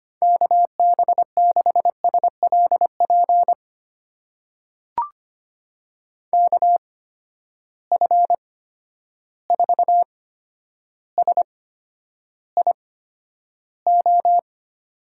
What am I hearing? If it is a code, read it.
Morse code, heard as a keyed tone, about 25 wpm, spaced as for 11 wpm, 700 Hz. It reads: KB6SLP KF4SIO
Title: USA Call Signs - 6 Characters - 8x Character Spacing - 25wpm